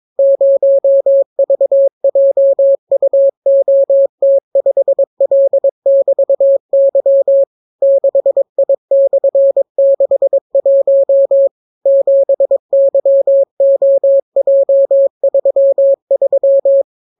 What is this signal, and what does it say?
Morse code, heard as a keyed tone, about 22 wpm, 550 Hz. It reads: 0VJUOT5L<BT>Y 6I/61 7YOJ33